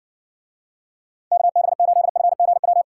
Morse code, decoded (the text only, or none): KB8FGW